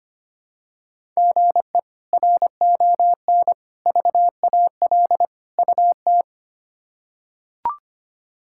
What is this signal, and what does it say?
Morse code, heard as a keyed tone, about 25 wpm, 700 Hz. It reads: GE RON VAL UT
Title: K1USN SST QSO Practice - 25wpm